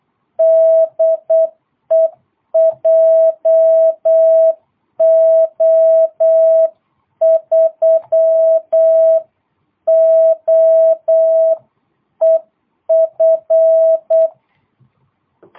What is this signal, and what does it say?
Morse code, heard as a keyed tone, about 8 wpm, 660 Hz. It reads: DE JO3OEF
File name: jh8cbh / 2022cw